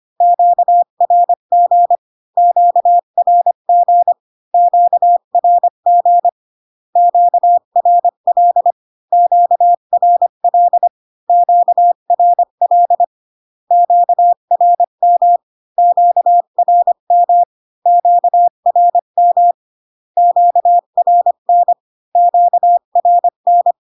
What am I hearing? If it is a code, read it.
Morse code, heard as a keyed tone, about 25 wpm, 700 Hz. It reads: QRG QRG QRG QRL QRL QRL QRM QRM QRM QRN QRN